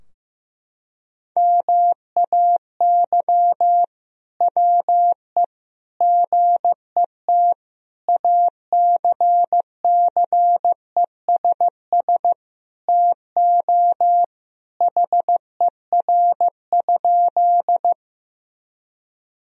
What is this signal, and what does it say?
Morse code, heard as a keyed tone, about 16 wpm, 700 Hz. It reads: MAY WE GET ACCESS TO HER?
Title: Sentences from Top 1000 Words - Review Part 4 - 15wpm